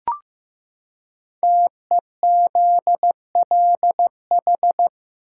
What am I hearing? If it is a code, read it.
Morse code, heard as a keyed tone, about 15 wpm, 700 Hz. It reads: TEZLH
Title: Code Groups - 5 Letters - 15wpm